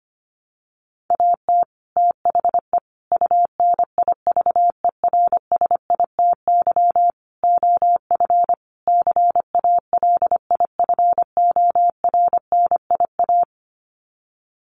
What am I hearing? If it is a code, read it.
Morse code, heard as a keyed tone, about 25 wpm, 700 Hz. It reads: AT THE UNIVERSITY OF CALIFORNIA
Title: Sets of 5 Words Encore2 25wpm